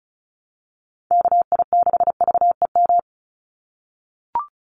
Morse code, heard as a keyed tone, about 34 wpm, 700 Hz. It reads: KI6VEM
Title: USA Call Signs 35wpm